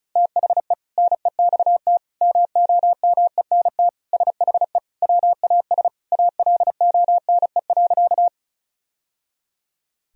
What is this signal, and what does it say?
Morse code, heard as a keyed tone, about 36 wpm, 700 Hz. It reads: THE NEXT MOMENT SHE WAS ALONE.
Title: Sentences from Top 1000 Words - Review Part 1 - 35wpm